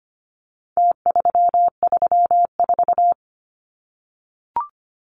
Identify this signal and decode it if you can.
Morse code, heard as a keyed tone, about 25 wpm, 700 Hz. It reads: T334
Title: Numbers - Contest Style - 4-Digit 25wpm